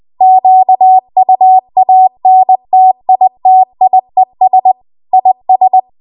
Morse code, heard as a keyed tone, about 20 wpm, 750 Hz. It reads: QUANTITIES IS
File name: Archive / 20 WPM / 20WPM